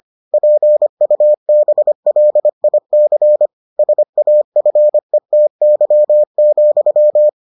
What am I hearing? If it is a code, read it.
Morse code, heard as a keyed tone, about 25 wpm, 600 Hz. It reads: PUBLIC SAFETY,